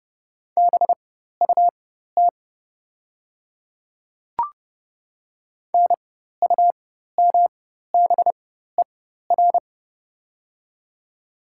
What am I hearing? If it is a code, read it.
Morse code, heard as a keyed tone, about 30 wpm, spaced as for 14 wpm, 700 Hz. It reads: BUT NUMBER
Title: 4x Character Spacing - Top 200 Words - 30wpm